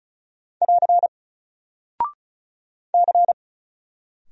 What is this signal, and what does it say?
Morse code, heard as a keyed tone, about 36 wpm, 700 Hz. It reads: <AR> C
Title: Single Character (Letter:Number:Punctuation:Prosign) 35wpm